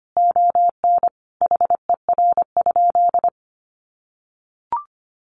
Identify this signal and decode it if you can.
Morse code, heard as a keyed tone, about 25 wpm, 700 Hz. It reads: ON HER?
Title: Sentences from Top 1500 Words 25wpm